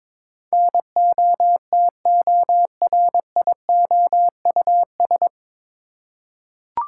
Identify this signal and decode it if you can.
Morse code, heard as a keyed tone, about 22 wpm, 700 Hz. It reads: NOTORIOUS
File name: 9 Letter Words - First 3 Letters Given - 22wpm